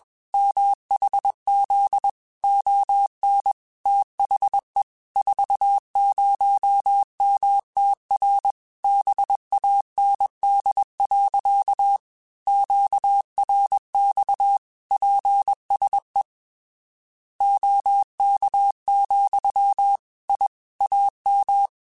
Morse code, heard as a keyed tone, about 21 wpm, 800 Hz. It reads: MHZ ON THE 40MTR BAND. QRX PSE OK, I AM
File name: QSOs / QSOs-mp3s / kc7haa@20